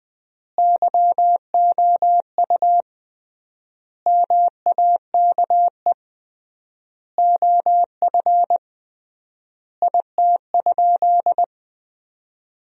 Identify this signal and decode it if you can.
Morse code, heard as a keyed tone, about 20 wpm, 700 Hz. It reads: YOU MAKE OF IT?
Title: 3x Word-Spacing - Sentences from Top 100 Words - 20wpm